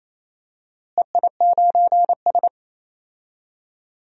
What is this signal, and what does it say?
Morse code, heard as a keyed tone, about 28 wpm, 700 Hz. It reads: EI9S
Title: DXCC-100 Call Signs - 4 Characters - Repeated 3x - 28wpm